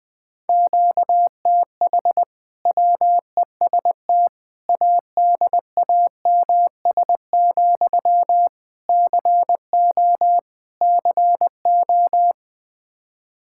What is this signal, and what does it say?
Morse code, heard as a keyed tone, about 20 wpm, 700 Hz. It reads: QTH WEST ADAMS, CO CO
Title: Practice QSO - Exchange 1 - Beginning - 20wpm